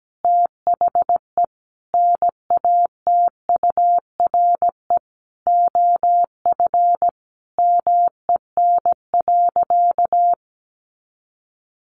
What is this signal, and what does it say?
Morse code, heard as a keyed tone, about 17 wpm, 700 Hz. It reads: THE NATURE OF MEN.
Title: Sentences from Top 1500 Words 17wpm